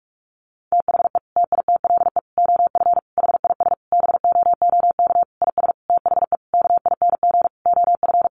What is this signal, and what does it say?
Morse code, heard as a keyed tone, about 44 wpm, 700 Hz. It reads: THE TITLE OF HIS BOOK IS THE KING OF
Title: Sentences from Top 1200 Words 45wpm